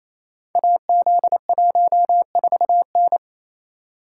Morse code, heard as a keyed tone, about 28 wpm, 700 Hz. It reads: AZ14N
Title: Code Groups - 5 Letter:Numbers - 28wpm